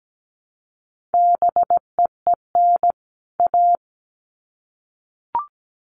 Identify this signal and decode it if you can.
Morse code, heard as a keyed tone, about 17 wpm, 700 Hz. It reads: BEEN A
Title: Sets of 2 Words 17wpm